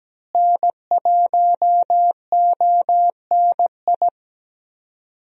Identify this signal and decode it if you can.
Morse code, heard as a keyed tone, about 17 wpm, 700 Hz. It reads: N1ONI
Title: USA Call Signs - 5 Characters - Repeated 3x - 17wpm